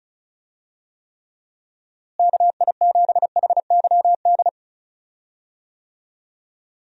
Morse code, heard as a keyed tone, about 34 wpm, 700 Hz. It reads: KI7HYD
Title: USA Call Signs - 6 Characters - Repeated 3x - 35wpm